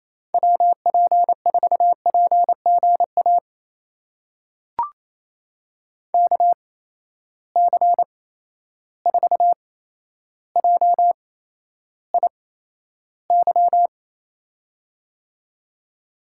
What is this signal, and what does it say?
Morse code, heard as a keyed tone, about 28 wpm, spaced as for 12 wpm, 700 Hz. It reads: WP4PGA KC4JIY
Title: USA Call Signs - 6 Characters - 8x Character Spacing - 28wpm